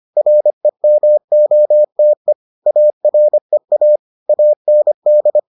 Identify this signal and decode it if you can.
Morse code, heard as a keyed tone, about 25 wpm, 600 Hz. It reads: REMOTE AREA AND